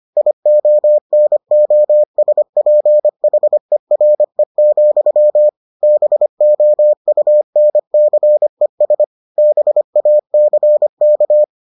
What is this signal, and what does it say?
Morse code, heard as a keyed tone, about 25 wpm, 600 Hz. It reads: IONOSPHERE, BOUNCES BACK